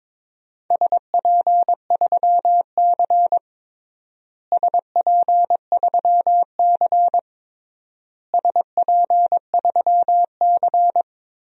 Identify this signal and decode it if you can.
Morse code, heard as a keyed tone, about 22 wpm, 700 Hz. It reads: SP3C SP3C SP3C
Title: DXCC-100 Call Signs - 4 Characters - Repeated 3x - 22wpm